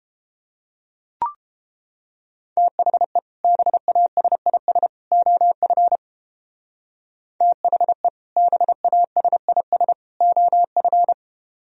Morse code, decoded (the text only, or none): THE BASIS OF THE BASIS OF